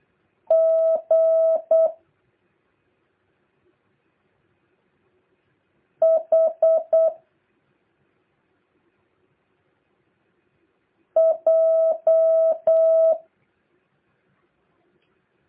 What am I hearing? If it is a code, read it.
Morse code, heard as a keyed tone, about 8 wpm, 650 Hz. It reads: G H J
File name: jh8cbh / 2023cw